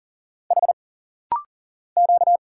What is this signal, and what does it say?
Morse code, heard as a keyed tone, about 40 wpm, 700 Hz. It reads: H Q
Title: Single Letters - Rapid-Fire 40wpm